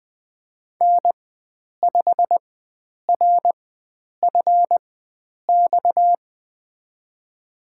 Morse code, heard as a keyed tone, about 20 wpm, spaced as for 9 wpm, 700 Hz. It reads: N5RFX
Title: USA Call Signs - 5 Characters - 4x Character Spacing - 20wpm